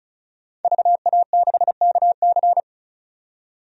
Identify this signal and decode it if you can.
Morse code, heard as a keyed tone, about 36 wpm, 700 Hz. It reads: VA6KC
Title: DXCC-100 Call Signs - 5 Characters - 35wpm